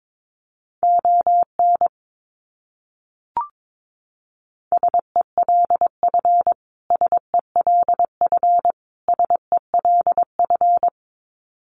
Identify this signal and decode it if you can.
Morse code, heard as a keyed tone, about 22 wpm, 700 Hz. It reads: ON SELF SELF SELF
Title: Top 300 Words Repeated 22wpm